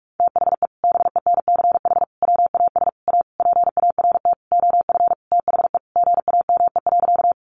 Morse code, heard as a keyed tone, about 44 wpm, 700 Hz. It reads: THE BENCH WAS A PART OF THE GAME.
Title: Sentences from Top 1400 Words 45wpm